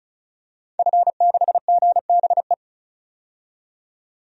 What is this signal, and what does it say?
Morse code, heard as a keyed tone, about 36 wpm, 700 Hz. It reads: F6GBE
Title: DXCC-100 Call Signs - 5 Characters - 35wpm